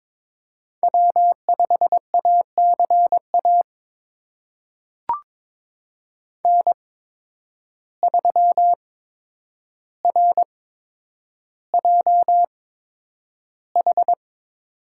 Morse code, heard as a keyed tone, about 22 wpm, spaced as for 9 wpm, 700 Hz. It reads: W5ACA N3RJH
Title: USA Call Signs - 8x Character Spacing - 22wpm